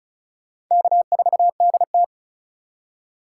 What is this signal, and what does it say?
Morse code, heard as a keyed tone, about 46 wpm, 700 Hz. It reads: K4DT